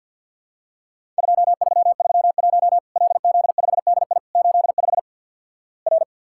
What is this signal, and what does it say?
Morse code, heard as a keyed tone, about 50 wpm, 700 Hz. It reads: 2331 LZ5DI 85 R
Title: Worked All Europe DX Contest - QTC Exchange with Repeats - 50wpm